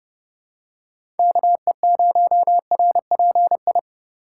Morse code, heard as a keyed tone, about 30 wpm, 700 Hz. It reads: KE0RPI
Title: USA Call Signs - 4x Character Spacing - 30wpm